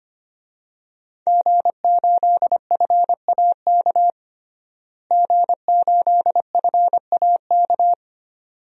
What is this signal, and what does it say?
Morse code, heard as a keyed tone, about 25 wpm, 700 Hz. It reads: G8FAK G8FAK
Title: DXCC-100 Call Signs - 5 Characters - Repeated 2x - 25wpm